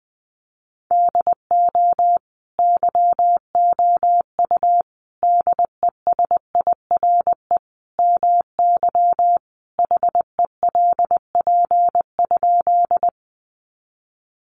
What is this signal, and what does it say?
Morse code, heard as a keyed tone, about 20 wpm, 700 Hz. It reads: DO YOU DESIRE MY HELP?
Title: Sentences from Top 1100 Words 20wpm